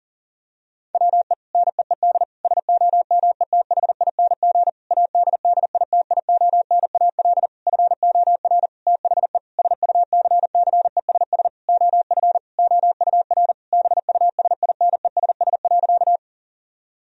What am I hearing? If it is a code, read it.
Morse code, heard as a keyed tone, about 40 wpm, 700 Hz. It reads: WE NEED SOMETHING ADDITIONAL FOR THE SUCCESS OF OUR BUSINESS.
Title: Sentences from Top 1000 Words - Review Part 4 - 40wpm